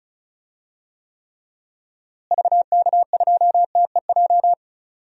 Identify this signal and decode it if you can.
Morse code, heard as a keyed tone, about 36 wpm, 700 Hz. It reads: VK2TEJ
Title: DXCC-100 Call Signs - 6 Characters - Repeated 2x - 35wpm